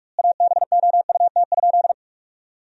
Morse code, heard as a keyed tone, about 46 wpm, 700 Hz. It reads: ABOUT?
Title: 2x Word-Spacing - Sentences from Top 100 Words - 45wpm